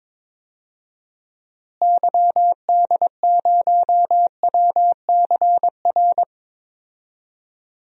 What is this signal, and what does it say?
Morse code, heard as a keyed tone, about 22 wpm, 700 Hz. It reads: YD0WCR
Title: DXCC-100 Call Signs - 6 Characters - Repeated 3x - 22wpm